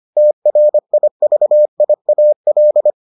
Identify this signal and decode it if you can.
Morse code, heard as a keyed tone, about 25 wpm, 600 Hz. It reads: TRIVIAL